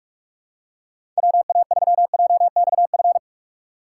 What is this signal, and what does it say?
Morse code, heard as a keyed tone, about 46 wpm, 700 Hz. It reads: WA3JXF
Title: USA Call Signs - 6 Characters - 4x Character Spacing - 45wpm